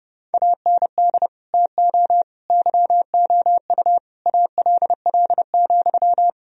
Morse code, heard as a keyed tone, about 29 wpm, 700 Hz. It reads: AND TO YOU ALL,
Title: Sentences from Top 1000 Words - Review Part 3 - 30wpm